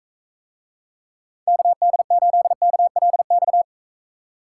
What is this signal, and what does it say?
Morse code, heard as a keyed tone, about 43 wpm, 700 Hz. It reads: KD8KLX